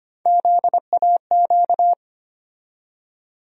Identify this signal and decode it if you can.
Morse code, heard as a keyed tone, about 25 wpm, 700 Hz. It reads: ZAQ
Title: USA Call Sign Postfixes - 3 Letters 25wpm